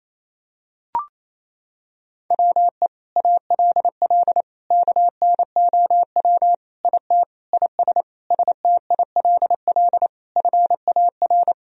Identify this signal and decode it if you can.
Morse code, heard as a keyed tone, about 28 wpm, 700 Hz. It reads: WE ALL KNOW IT IS STILL FAR
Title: Sentences from Top 1000 Words - Review Part 1 - 28wpm